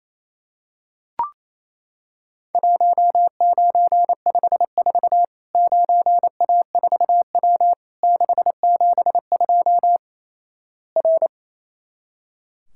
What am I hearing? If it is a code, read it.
Morse code, heard as a keyed tone, about 27 wpm, 700 Hz. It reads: 1954 9A4W 672 R
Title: Worked All Europe DX Contest - QTC Exchange - 28wpm